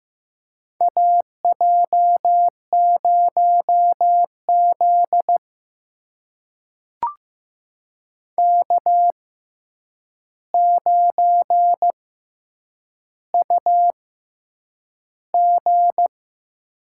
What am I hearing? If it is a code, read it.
Morse code, heard as a keyed tone, about 15 wpm, spaced as for 8 wpm, 700 Hz. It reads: AJ0Z K9UG
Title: USA Call Signs - 4 Characters - 6x Character Spacing - 15wpm